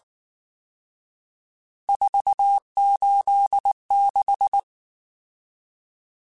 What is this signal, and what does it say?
Morse code, heard as a keyed tone, about 19 wpm, 800 Hz. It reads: 486
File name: RSTs / RST-mp3s / Various RST @18